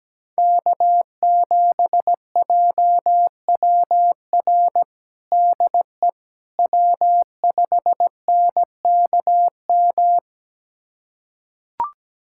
Morse code, heard as a keyed tone, about 17 wpm, 700 Hz. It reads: K7JWR DE W5NKM
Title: USA Callsign DE Callsign 17wpm